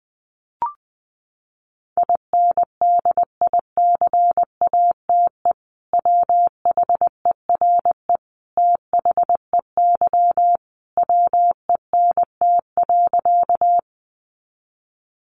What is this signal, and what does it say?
Morse code, heard as a keyed tone, about 20 wpm, 700 Hz. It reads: INDICATE WHERE THEY WENT.
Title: Sentences from Top 1300 Words 20wpm